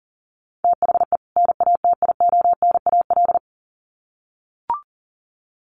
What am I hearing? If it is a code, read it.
Morse code, heard as a keyed tone, about 37 wpm, 700 Hz. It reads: THE NATIONAL
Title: Sets of 2 Words 40wpm